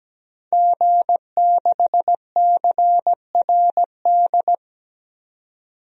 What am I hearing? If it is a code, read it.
Morse code, heard as a keyed tone, about 17 wpm, 700 Hz. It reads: G6CRD
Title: DXCC-100 Call Signs - 5 Characters - 17wpm